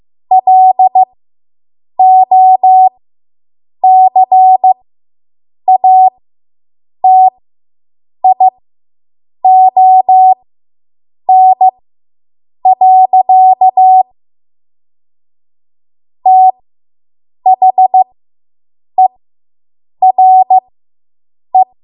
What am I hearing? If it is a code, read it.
Morse code, heard as a keyed tone, about 15 wpm, spaced as for 7 wpm, 750 Hz. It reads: LOCATION. THERE